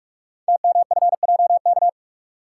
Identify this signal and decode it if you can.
Morse code, heard as a keyed tone, about 44 wpm, 700 Hz. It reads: TMFJK
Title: Code Groups - 5 Letter:Numbers - 45wpm